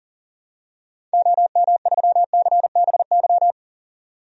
Morse code, heard as a keyed tone, about 40 wpm, 700 Hz. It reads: OM3CBY
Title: DXCC-100 Call Signs - 6 Characters - 40wpm